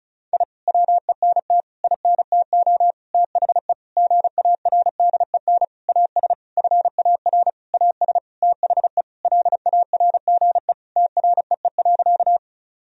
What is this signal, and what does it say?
Morse code, heard as a keyed tone, about 34 wpm, 700 Hz. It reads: I WENT INTO THE GARDEN AS FAR AS THE LARGE TREE.
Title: Sentences from Top 1000 Words - Review Part 4 - 35wpm